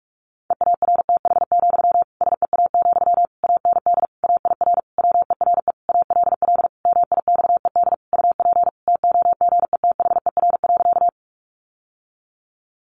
Morse code, heard as a keyed tone, about 44 wpm, 700 Hz. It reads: EARTH, SEA, AND AIR WERE ALL MIXED UP TOGETHER.
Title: Sentences from Top 1500 Words 45wpm